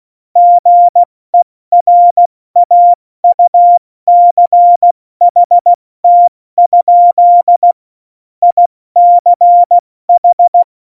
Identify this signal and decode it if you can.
Morse code, heard as a keyed tone, about 16 wpm, 700 Hz. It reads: GERAUCHT? ICH
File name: Morsekurs / Witze / Witz 08. Dezember 2020, 70 F80